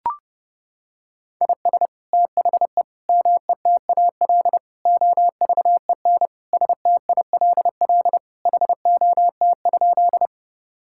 Morse code, heard as a keyed tone, about 32 wpm, 700 Hz. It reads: IS THE METAL OVEN STILL HOT?
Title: Sentences from Top 1000 Words - Review Part 6 - 30wpm